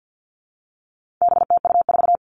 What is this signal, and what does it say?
Morse code, heard as a keyed tone, about 50 wpm, 700 Hz. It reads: BTU4
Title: All Cut Numbers - 4-Digit 50wpm